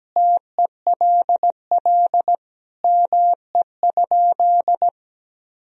Morse code, heard as a keyed tone, about 17 wpm, 700 Hz. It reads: TELL ME?
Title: Sentences from Top 1000 Words - Review Part 3 - 17wpm